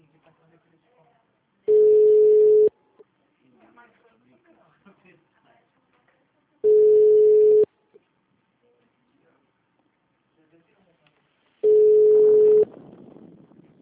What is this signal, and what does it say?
Ringback tone (Europe, 425 Hz), with speech.